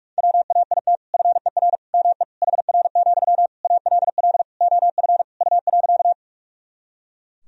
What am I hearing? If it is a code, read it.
Morse code, heard as a keyed tone, about 41 wpm, 700 Hz. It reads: WAIT FER ME HR, ALL OF U.
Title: Sentences from Top 500 Words and Abbreviations 45wpm